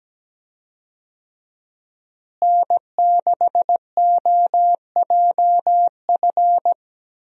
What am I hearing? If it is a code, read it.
Morse code, heard as a keyed tone, about 17 wpm, 700 Hz. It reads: N6OJF
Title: USA Call Signs - 5 Characters - Repeated 2x - 17wpm